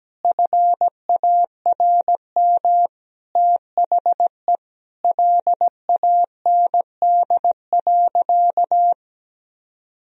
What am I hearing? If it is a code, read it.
Morse code, heard as a keyed tone, about 17 wpm, 700 Hz. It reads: FARM THE LAND.
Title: Sentences from Top 1000 Words - Review Part 2 - 17wpm